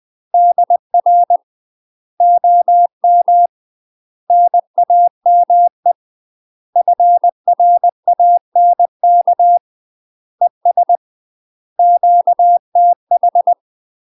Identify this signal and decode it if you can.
Morse code, heard as a keyed tone, about 20 wpm, 700 Hz. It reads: DR OM NAME FRANK ES QTH
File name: 20WPM